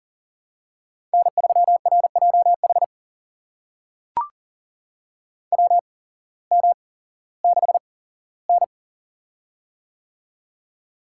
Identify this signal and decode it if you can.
Morse code, heard as a keyed tone, about 40 wpm, spaced as for 19 wpm, 700 Hz. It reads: N3RJH WM6N
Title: USA Call Signs - 8x Character Spacing - 40wpm